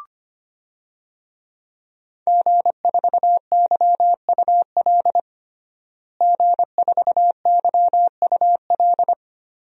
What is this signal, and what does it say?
Morse code, heard as a keyed tone, about 25 wpm, 700 Hz. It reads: G4YUL G4YUL
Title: DXCC-100 Call Signs - 5 Characters - Repeated 3x - 25wpm